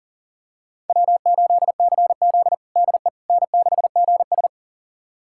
Morse code, heard as a keyed tone, about 40 wpm, 700 Hz. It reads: W8CZ DE N6GS
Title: USA Callsign DE Callsign - 4 Characters - 40wpm